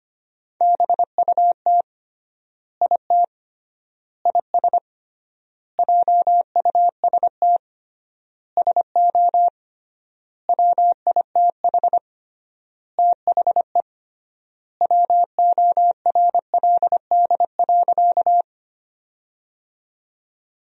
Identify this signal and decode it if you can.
Morse code, heard as a keyed tone, about 25 wpm, 700 Hz. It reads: BUT IT IS JUST SO WITH THE WORLD.
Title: 3x Word-Spacing - Sentences from Top 200 Words - 25wpm